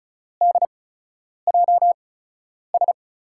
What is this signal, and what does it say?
Morse code, heard as a keyed tone, about 36 wpm, spaced as for 10 wpm, 700 Hz. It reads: DJS
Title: USA Call Signs - 6 Characters - 8x Character Spacing - 35wpm